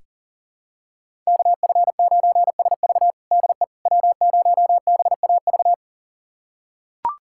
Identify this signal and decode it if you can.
Morse code, heard as a keyed tone, about 40 wpm, 700 Hz. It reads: KF9SV DE W0BAV
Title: USA Callsign DE Callsign - 5 Characters - 40wpm